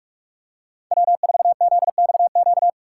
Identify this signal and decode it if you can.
Morse code, heard as a keyed tone, about 46 wpm, 700 Hz. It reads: W4ZXQ
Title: USA Call Signs - 5 Characters - 4x Character Spacing - 45wpm